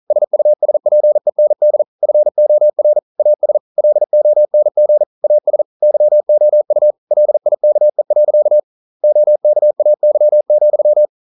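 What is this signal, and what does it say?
Morse code, heard as a keyed tone, about 41 wpm, 600 Hz. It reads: SUSPEND FOR AS LONG AS YOU LIKE. OKAY,
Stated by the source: Morse code practice at 41 wpm, from CW med Gnister.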